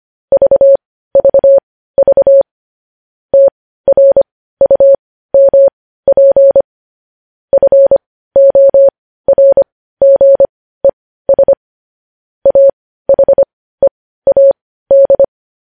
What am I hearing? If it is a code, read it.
Morse code, heard as a keyed tone, about 25 wpm, spaced as for 15 wpm, 550 Hz. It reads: VVV TRUMP FORGES AHEAD
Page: news-headlines-in-morse-code-at-15-wpm